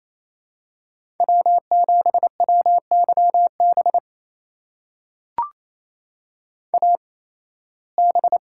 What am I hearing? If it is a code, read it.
Morse code, heard as a keyed tone, about 27 wpm, spaced as for 18 wpm, 700 Hz. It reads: W7WYB AB
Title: USA Call Signs - 5 Characters - 8x Character Spacing - 28wpm